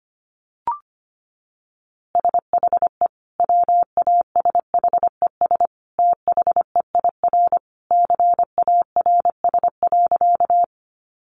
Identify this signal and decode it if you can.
Morse code, heard as a keyed tone, about 25 wpm, 700 Hz. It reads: SHE WASHES THEIR CARS.
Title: Easy Phrases 25wpm